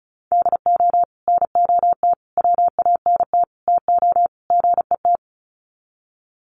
Morse code, heard as a keyed tone, about 34 wpm, 700 Hz. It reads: DO NOT WANT TO GET
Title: Sets of 5 Words 35wpm